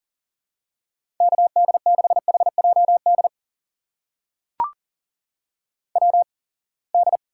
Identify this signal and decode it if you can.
Morse code, heard as a keyed tone, about 40 wpm, spaced as for 27 wpm, 700 Hz. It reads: KD6HJD WD
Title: USA Call Signs - 6 Characters - 8x Character Spacing - 40wpm